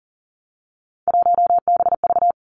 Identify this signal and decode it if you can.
Morse code, heard as a keyed tone, about 40 wpm, 700 Hz. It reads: 1BV
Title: Three Challenging Letter-Number Combinations 40wpm